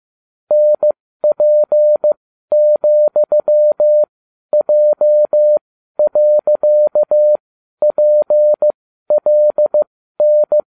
Morse code, heard as a keyed tone, about 15 wpm, spaced as for 12 wpm, 600 Hz. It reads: NP,J.PLN
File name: cw / ubacw20 2